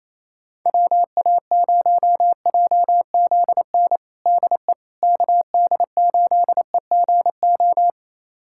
Morse code, heard as a keyed tone, about 27 wpm, 700 Hz. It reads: WA0JZN DE KD8EGO
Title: USA Callsign DE Callsign 28wpm